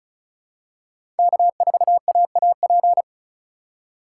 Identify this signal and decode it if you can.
Morse code, heard as a keyed tone, about 36 wpm, 700 Hz. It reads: K4AAP
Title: USA Call Signs - 5 Characters - 3x Character Spacing - 35wpm